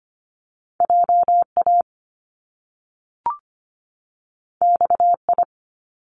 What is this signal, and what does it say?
Morse code, heard as a keyed tone, about 25 wpm, 700 Hz. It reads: JA XI
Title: Top Two Letter Combinations in Words 25wpm